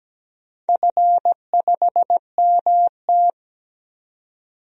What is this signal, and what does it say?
Morse code, heard as a keyed tone, about 17 wpm, 700 Hz. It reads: F5MT
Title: DXCC-100 Call Signs - 4 Characters - 17wpm